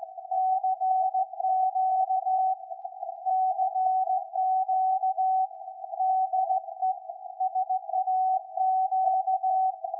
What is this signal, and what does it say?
Morse code, heard as a keyed tone, about 16 wpm, 750 Hz. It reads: CQ CQ DE SAQ
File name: iz7slz / VLF / SAQ 13-02-2023